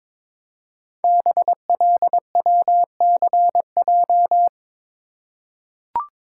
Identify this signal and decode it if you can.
Morse code, heard as a keyed tone, about 22 wpm, 700 Hz. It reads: BLWCJ
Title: Code Groups - 5 Letters - 22wpm